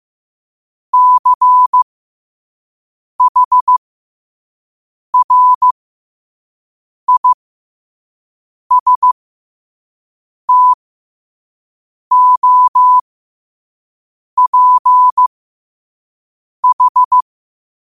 Morse code, heard as a keyed tone, about 15 wpm, spaced as for 5 wpm, 1000 Hz. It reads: CHRISTOPH
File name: qotdfiles / 5wpm